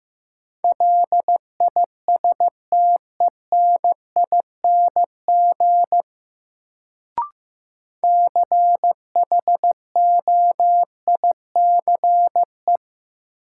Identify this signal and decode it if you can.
Morse code, heard as a keyed tone, about 15 wpm, 700 Hz. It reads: LISTENING CHOICE
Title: Sets of 5 Words - New Word Drill - 15wpm